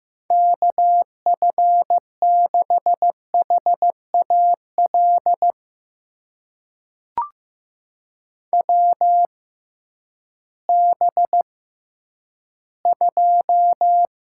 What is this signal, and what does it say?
Morse code, heard as a keyed tone, about 15 wpm, spaced as for 10 wpm, 700 Hz. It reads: KF6HAL WB2
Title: USA Call Signs - 6 Characters - 6x Character Spacing - 15wpm